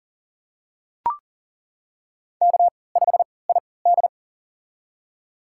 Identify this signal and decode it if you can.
Morse code, heard as a keyed tone, about 40 wpm, spaced as for 23 wpm, 700 Hz. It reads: K5ID
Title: USA Call Signs - 4 Characters - 3x Character Spacing - 40wpm